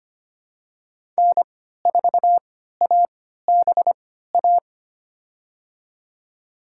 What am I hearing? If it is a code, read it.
Morse code, heard as a keyed tone, about 25 wpm, spaced as for 14 wpm, 700 Hz. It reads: N4ABA
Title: USA Call Signs - 5 Characters - 3x Character Spacing - 25wpm